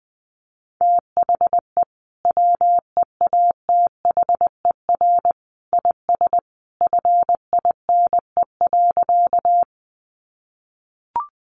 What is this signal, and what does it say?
Morse code, heard as a keyed tone, about 20 wpm, 700 Hz. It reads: THE WEATHER IS FINE.
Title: Sentences from Top 800 Words 20wpm